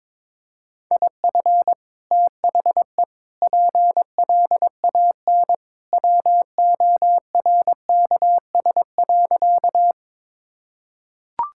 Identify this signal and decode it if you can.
Morse code, heard as a keyed tone, about 22 wpm, 700 Hz. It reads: IF THE PLAN WORKS.
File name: practice / 22wpm / Sentences with 2-Word Phrases - 22wpm